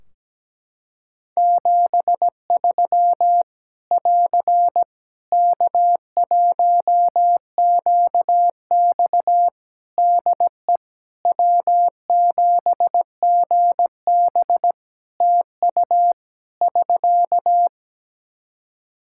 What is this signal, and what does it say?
Morse code, heard as a keyed tone, about 17 wpm, 700 Hz. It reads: 73 <AR> K1QX DE W7GB TU <SK>
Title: K1USN SST QSO Practice 17wpm